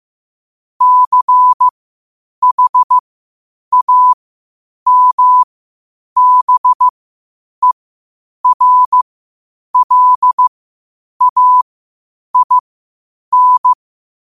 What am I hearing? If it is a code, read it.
Morse code, heard as a keyed tone, about 15 wpm, spaced as for 8 wpm, 1000 Hz. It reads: CHAMBERLAIN
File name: qotdfiles / 8wpm